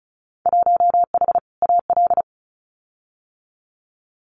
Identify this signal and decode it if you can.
Morse code, heard as a keyed tone, about 36 wpm, 700 Hz. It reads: 1H AL
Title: Winter Field Day Exchanges 35wpm